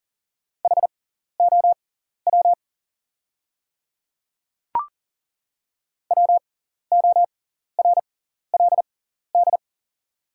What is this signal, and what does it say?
Morse code, heard as a keyed tone, about 40 wpm, spaced as for 14 wpm, 700 Hz. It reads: HOW WORLD
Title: 6x Character Spacing - Top 200 Words - 40wpm